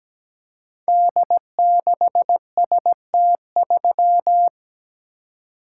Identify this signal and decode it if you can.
Morse code, heard as a keyed tone, about 17 wpm, 700 Hz. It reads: D6ST3
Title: Code Groups - 5 Letter:Numbers - 17wpm